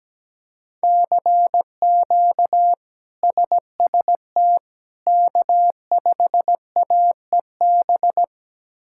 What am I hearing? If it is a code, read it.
Morse code, heard as a keyed tone, about 17 wpm, 700 Hz. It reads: CQ SST K5AEB
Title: K1USN SST QSO Practice - 17wpm